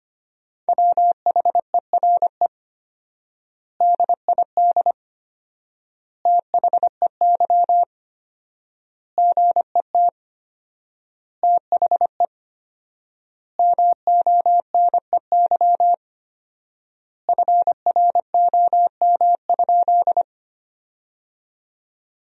Morse code, heard as a keyed tone, about 25 wpm, 700 Hz. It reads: WHERE DID THEY GET THE MONEY FROM?
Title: 4x Word-Spacing - Sentences from Top 400 Words - 25wpm